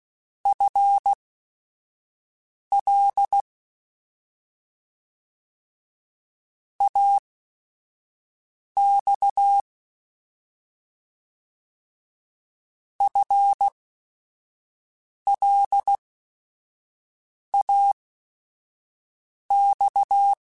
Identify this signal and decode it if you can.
Morse code, heard as a keyed tone, about 16 wpm, spaced as for 5 wpm, 800 Hz. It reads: FL AX FLAX
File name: FL / FL mp3s / Fast FL @5